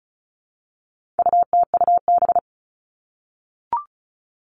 Morse code, heard as a keyed tone, about 34 wpm, 700 Hz. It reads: UTUB
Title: All Cut Numbers - 4-Digit 35wpm